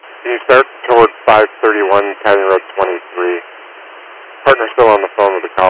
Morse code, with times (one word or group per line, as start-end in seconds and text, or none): none